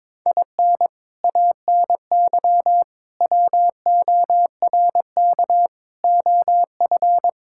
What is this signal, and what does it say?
Morse code, heard as a keyed tone, about 22 wpm, 700 Hz. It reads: IN ANY WORK OF
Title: Sentences from Top 1000 Words - Review Part 6 - 22wpm